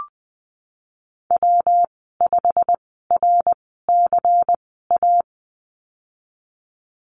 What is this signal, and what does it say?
Morse code, heard as a keyed tone, about 20 wpm, spaced as for 15 wpm, 700 Hz. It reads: W5RCA
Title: 186 Lesson 30g - Letter b - Call Signs 20wpm